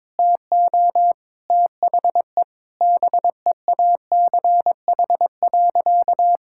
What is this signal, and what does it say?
Morse code, heard as a keyed tone, about 21 wpm, 700 Hz. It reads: TO THE BEACH.
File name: practice / 22wpm / Sentences with 4-Word Phrases - 22wpm